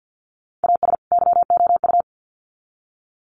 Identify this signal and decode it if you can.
Morse code, heard as a keyed tone, about 50 wpm, 700 Hz. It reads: AS YOU
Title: Sets of 2 Words 50wpm